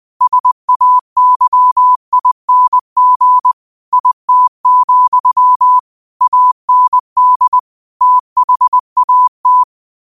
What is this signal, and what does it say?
Morse code, heard as a keyed tone, about 20 wpm, 1000 Hz. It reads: SAYING IT, AND THAT
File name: 20wpm